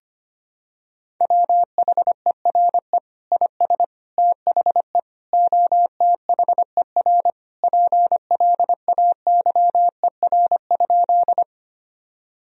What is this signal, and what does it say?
Morse code, heard as a keyed tone, about 25 wpm, 700 Hz. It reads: WHERE IS THE OTHER PLAYER?
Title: Sentences from Top 1000 Words - Review Part 4 - 25wpm